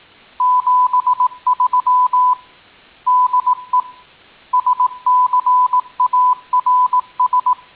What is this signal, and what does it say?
Morse code, heard as a keyed tone, about 18 wpm, 1000 Hz. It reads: 73 DE SCARS